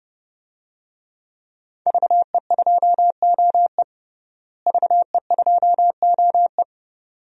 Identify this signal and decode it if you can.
Morse code, heard as a keyed tone, about 30 wpm, 700 Hz. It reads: VE2OE VE2OE
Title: DXCC-100 Call Signs - 5 Characters - Repeated 2x - 30wpm